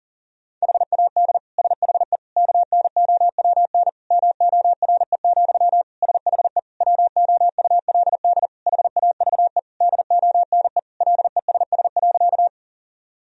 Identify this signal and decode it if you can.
Morse code, heard as a keyed tone, about 37 wpm, 700 Hz. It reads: HAD SHE KNOWN MORE, SHE WOULD HAVE DONE LESS.
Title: Sentences from Top 1000 Words - Review Part 3 - 40wpm